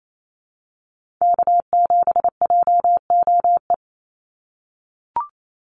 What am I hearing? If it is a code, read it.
Morse code, heard as a keyed tone, about 28 wpm, 700 Hz. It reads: K7JOE
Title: Call Signs - 28wpm